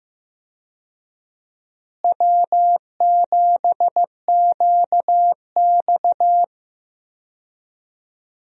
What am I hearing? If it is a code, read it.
Morse code, heard as a keyed tone, about 15 wpm, 700 Hz. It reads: W7QX
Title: USA Call Signs - 4 Characters - Repeated 2x - 15wpm